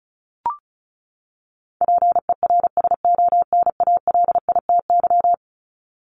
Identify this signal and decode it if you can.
Morse code, heard as a keyed tone, about 34 wpm, 700 Hz. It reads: PERSONALITY
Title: Top 1251-1500 Words 35wpm